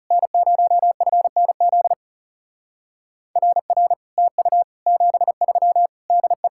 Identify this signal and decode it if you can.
Morse code, heard as a keyed tone, about 36 wpm, 700 Hz. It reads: N0FNZ RR TU 73 DE
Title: Speed-Racing Common QSO Phrases Encore - 30wpm